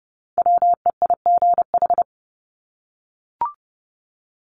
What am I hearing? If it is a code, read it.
Morse code, heard as a keyed tone, about 30 wpm, 700 Hz. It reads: WEIGH
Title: Top 1751-2000 Words 30wpm